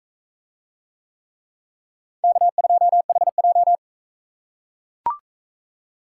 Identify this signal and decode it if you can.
Morse code, heard as a keyed tone, about 41 wpm, 700 Hz. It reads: K2HJ